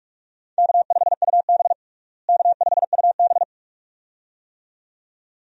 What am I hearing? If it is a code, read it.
Morse code, heard as a keyed tone, about 41 wpm, 700 Hz. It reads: K5UB K5UB
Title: USA Call Signs - 4 Characters - Repeated 2x - 45wpm